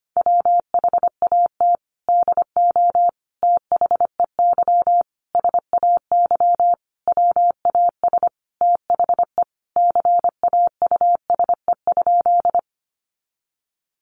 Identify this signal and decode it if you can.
Morse code, heard as a keyed tone, about 25 wpm, 700 Hz. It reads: WHAT DO THEY SAY WAS THE CAUSE?
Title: Sentences from Top 200 Words 25wpm